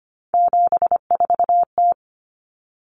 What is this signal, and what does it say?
Morse code, heard as a keyed tone, about 25 wpm, 700 Hz. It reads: 74T
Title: Common Cut Numbers - 3-Digit 25wpm